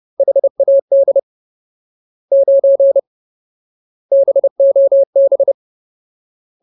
Morse code, heard as a keyed tone, about 30 wpm, 550 Hz. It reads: HAD 9 BOB
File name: Ultimate Word List / 30WPM Ultimate